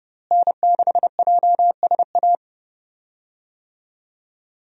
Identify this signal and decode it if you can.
Morse code, heard as a keyed tone, about 30 wpm, 700 Hz. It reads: N6JSA
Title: USA Call Signs - 5 Characters - Repeated 3x - 30wpm